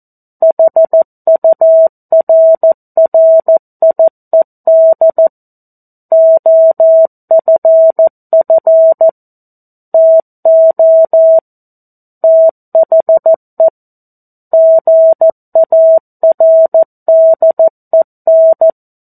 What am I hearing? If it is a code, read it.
Morse code, heard as a keyed tone, about 14 wpm, 650 Hz. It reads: HURRIED OFF TO THE GARDEN
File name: 02-Chapter-02 14 WPM